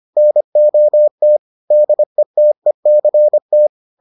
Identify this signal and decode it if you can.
Morse code, heard as a keyed tone, about 25 wpm, 600 Hz. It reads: NOT DETECT